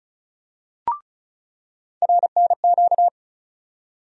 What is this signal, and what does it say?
Morse code, heard as a keyed tone, about 36 wpm, 700 Hz. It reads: RNQ
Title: USA Call Sign Postfixes - 3 Letters 35wpm